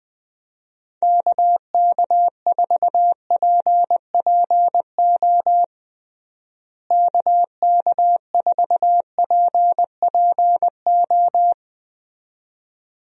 Morse code, heard as a keyed tone, about 20 wpm, 700 Hz. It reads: KK4PPO KK4PPO
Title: USA Call Signs - 6 Characters - Repeated 2x - 20wpm